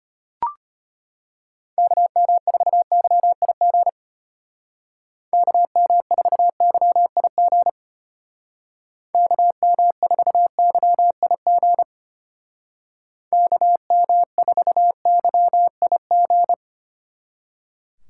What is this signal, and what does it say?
Morse code, heard as a keyed tone, about 32 wpm, 700 Hz. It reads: KM4YIG KM4YIG KM4YIG KM4YIG